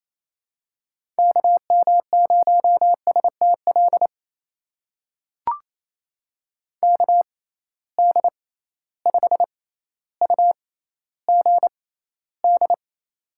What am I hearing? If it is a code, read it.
Morse code, heard as a keyed tone, about 27 wpm, spaced as for 14 wpm, 700 Hz. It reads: KM0STL KD5UGD
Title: USA Call Signs - 6 Characters - 6x Character Spacing - 28wpm